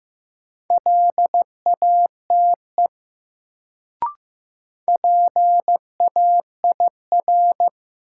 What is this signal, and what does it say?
Morse code, heard as a keyed tone, about 15 wpm, 700 Hz. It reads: LATE PAIR
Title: Sets of 3 Words - New Word Drill - 15wpm